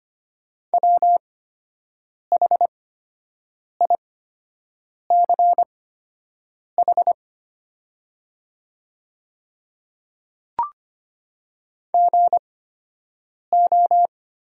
Morse code, heard as a keyed tone, about 25 wpm, spaced as for 7 wpm, 690 Hz. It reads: WHICH GO
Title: 8x Character Spacing - Top 100 Words - 25wpm